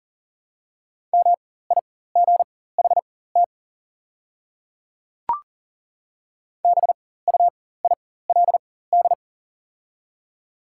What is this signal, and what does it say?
Morse code, heard as a keyed tone, about 40 wpm, spaced as for 19 wpm, 700 Hz. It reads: MIGHT BUILD
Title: 4x Character Spacing - Top 300 Words - 40wpm